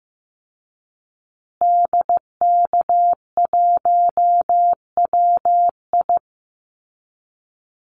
Morse code, heard as a keyed tone, about 15 wpm, 700 Hz. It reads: DK1WI
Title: Call Signs Repeated 2x - 15wpm